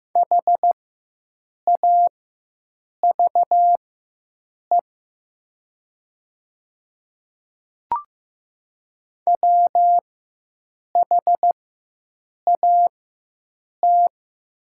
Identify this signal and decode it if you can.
Morse code, heard as a keyed tone, about 15 wpm, spaced as for 7 wpm, 700 Hz. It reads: HAVE WHAT
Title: 4x Character Spacing - Top 100 Words - 15wpm